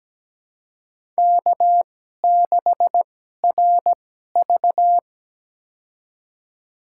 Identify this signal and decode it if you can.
Morse code, heard as a keyed tone, about 17 wpm, spaced as for 12 wpm, 700 Hz. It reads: K6RV
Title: USA Call Signs - 4 Characters - 2x Character Spacing - 17wpm